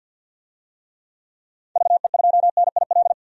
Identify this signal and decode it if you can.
Morse code, heard as a keyed tone, about 52 wpm, 700 Hz. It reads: VE2NIL
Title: DXCC-100 Call Signs - 6 Characters - 50wpm